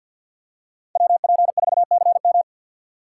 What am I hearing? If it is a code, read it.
Morse code, heard as a keyed tone, about 52 wpm, 700 Hz. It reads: WP4CM
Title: USA Call Signs - 5 Characters - 2x Character Spacing - 50wpm